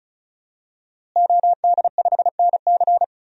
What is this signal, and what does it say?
Morse code, heard as a keyed tone, about 36 wpm, 700 Hz. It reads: OD5NC
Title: DXCC-100 Call Signs - 5 Characters - Repeated 2x - 35wpm